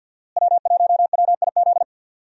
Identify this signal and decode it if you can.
Morse code, heard as a keyed tone, about 46 wpm, 700 Hz. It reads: W1WIZ
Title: USA Call Signs - 5 Characters - Repeated 3x - 50wpm